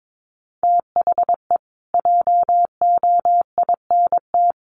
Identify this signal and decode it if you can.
Morse code, heard as a keyed tone, about 22 wpm, 700 Hz. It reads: THE JOINT
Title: Sets of 5 Words Encore2 22wpm